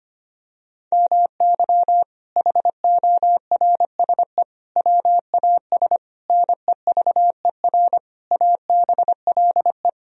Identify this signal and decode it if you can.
Morse code, heard as a keyed tone, about 25 wpm, 700 Hz. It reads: MY HORSE WAS NEVER ABLE
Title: Sentences from Top 1000 Words - Review Part 2 - 25wpm